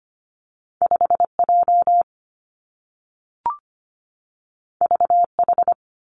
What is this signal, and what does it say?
Morse code, heard as a keyed tone, about 25 wpm, 700 Hz. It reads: 5J VH
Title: Two Challenging Letter-Number Combinations 25wpm